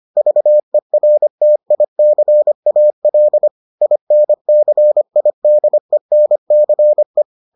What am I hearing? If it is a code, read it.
Morse code, heard as a keyed tone, about 25 wpm, 600 Hz. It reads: VERTICAL INCIDENCE